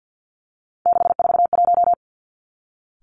Morse code, heard as a keyed tone, about 50 wpm, 700 Hz. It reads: 641
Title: Three Challenging Letter-Number Combinations 50wpm